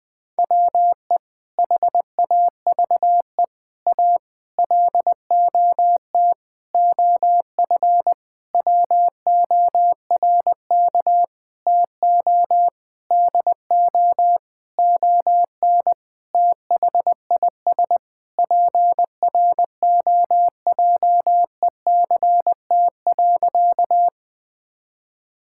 Morse code, heard as a keyed tone, about 20 wpm, 700 Hz. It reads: WE HAVE A LOT OF WORK TO DO ON THIS PROJECT.
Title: Sentences with 6-Word Phrases - Spoken Beginning - 20wpm